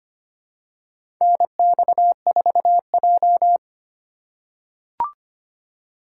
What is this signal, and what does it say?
Morse code, heard as a keyed tone, about 25 wpm, 700 Hz. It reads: NX4J
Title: USA Call Signs - 4 Characters - 4x Character Spacing - 25wpm